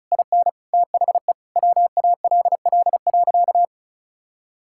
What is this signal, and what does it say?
Morse code, heard as a keyed tone, about 36 wpm, 700 Hz. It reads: IN THE WALL.
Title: Sentences from Top 1600 Words 35wpm